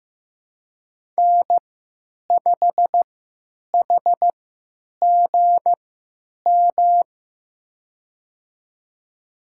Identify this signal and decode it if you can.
Morse code, heard as a keyed tone, about 15 wpm, spaced as for 9 wpm, 700 Hz. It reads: N5HGM
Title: USA Call Signs - 5 Characters - 3x Character Spacing - 15wpm